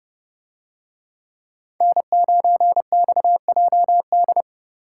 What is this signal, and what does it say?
Morse code, heard as a keyed tone, about 30 wpm, 700 Hz. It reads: N9XJD